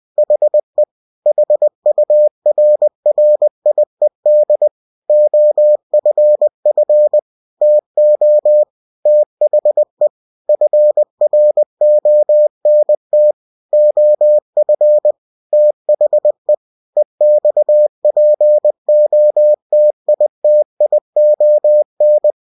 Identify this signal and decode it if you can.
Morse code, heard as a keyed tone, about 20 wpm, 600 Hz. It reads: HE HURRIED OFF TO THE FRONT OF THE EXPOTITION
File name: winnie-20wpm